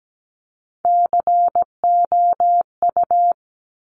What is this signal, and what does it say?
Morse code, heard as a keyed tone, about 17 wpm, 700 Hz. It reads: COU
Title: Top Three Letter Combinations in Words 17wpm